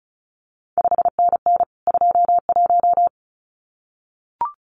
Morse code, heard as a keyed tone, about 36 wpm, 700 Hz. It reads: 5NN 21
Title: CQ WW DX Contest Exchange Encore - 35wpm